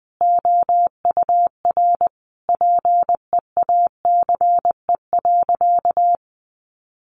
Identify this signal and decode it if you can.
Morse code, heard as a keyed tone, about 20 wpm, 700 Hz. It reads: OUR PEACE.
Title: Sentences from Top 1500 Words 20wpm